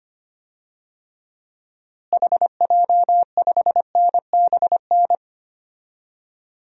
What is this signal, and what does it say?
Morse code, heard as a keyed tone, about 25 wpm, 700 Hz. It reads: HJ5NBN
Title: DXCC-100 Call Signs - 6 Characters - Repeated 2x - 25wpm